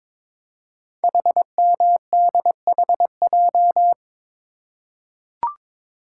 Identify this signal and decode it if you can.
Morse code, heard as a keyed tone, about 23 wpm, 700 Hz. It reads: HMDHJ
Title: Code Groups - 5 Letters - 22wpm